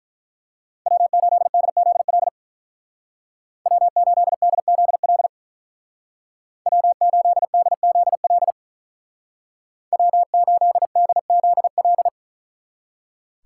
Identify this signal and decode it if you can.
Morse code, heard as a keyed tone, about 44 wpm, 700 Hz. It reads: W8DZL W8DZL W8DZL W8DZL
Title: Speed Racing - USA 1x3 Call Signs 35wpm